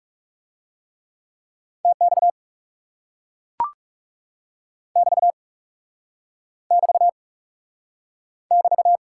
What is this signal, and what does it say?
Morse code, heard as a keyed tone, about 37 wpm, 700 Hz. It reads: TX <BT> <BT> <BT>
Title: Speed-Racing Top 90 QSO Elements 30wpm